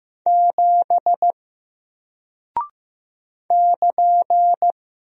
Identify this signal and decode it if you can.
Morse code, heard as a keyed tone, about 15 wpm, 700 Hz. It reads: 7 <KN>
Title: Single Character (Letter:Number:Punctuation:Prosign) Alt 15wpm